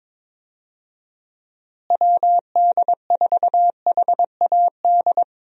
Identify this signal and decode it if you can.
Morse code, heard as a keyed tone, about 22 wpm, 700 Hz. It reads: WD4HAD
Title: USA Call Signs - 6 Characters - Repeated 3x - 22wpm